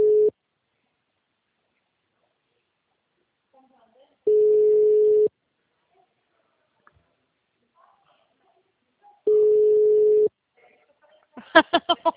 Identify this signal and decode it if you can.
Ringback tone (Europe, 425 Hz), with speech.
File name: out- -2013- - - .40128